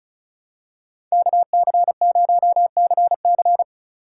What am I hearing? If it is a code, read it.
Morse code, heard as a keyed tone, about 34 wpm, 700 Hz. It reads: KC0CC
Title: USA Call Signs - 3x Character Spacing - 35wpm